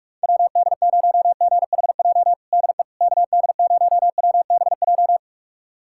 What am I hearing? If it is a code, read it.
Morse code, heard as a keyed tone, about 46 wpm, 700 Hz. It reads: WD0GHJ DE KD0WBJ
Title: USA Callsign DE Callsign - 6 Characters - 45wpm